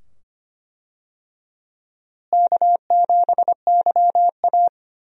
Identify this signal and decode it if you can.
Morse code, heard as a keyed tone, about 25 wpm, 700 Hz. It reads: K7YA
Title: USA Call Signs - 4 Characters - Repeated 2x - 25wpm